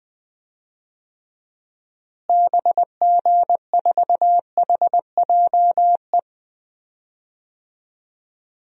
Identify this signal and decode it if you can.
Morse code, heard as a keyed tone, about 20 wpm, 700 Hz. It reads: BG4HJE
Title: DXCC-100 Call Signs - 6 Characters - Repeated 3x - 20wpm